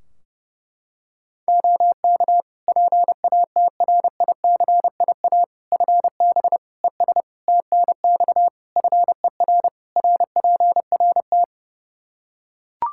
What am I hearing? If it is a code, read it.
Morse code, heard as a keyed tone, about 30 wpm, 700 Hz. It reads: OK PATRICIA FB ES TNX FER RPRT
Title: Practice QSO - Exchange 2 - Info - 30wpm